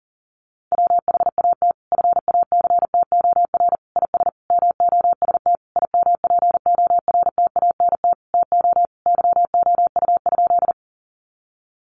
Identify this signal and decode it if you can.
Morse code, heard as a keyed tone, about 40 wpm, 700 Hz. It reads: WHAT FACTOR IS MOST IMPORTANT TO YOU?
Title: Sentences from Top 1300 Words 40wpm